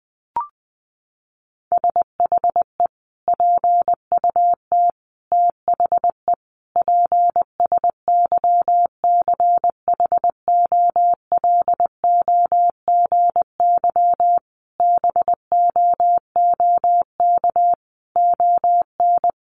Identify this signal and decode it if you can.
Morse code, heard as a keyed tone, about 20 wpm, 700 Hz. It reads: SHE PUT THE PSYCHOLOGY BOOK ON
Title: Sentences from Top 1400 Words 20wpm